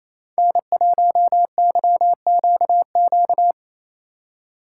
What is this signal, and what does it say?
Morse code, heard as a keyed tone, about 28 wpm, 700 Hz. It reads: N1YQQ
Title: Speed Racing - USA 1x3 Call Signs 28wpm